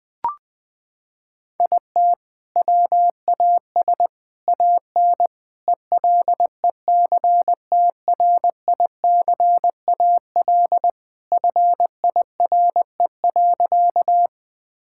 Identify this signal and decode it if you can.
Morse code, heard as a keyed tone, about 19 wpm, 700 Hz. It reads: IT WAS AN ELECTRICAL FIRE.
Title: Sentences from Top 1000 Words - Review Part 6 - 20wpm